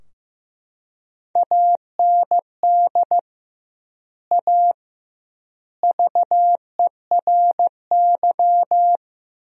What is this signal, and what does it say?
Morse code, heard as a keyed tone, about 15 wpm, 700 Hz. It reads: AND A VERY
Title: 2x Word-Spacing - Sentences from Top 200 Words - 15wpm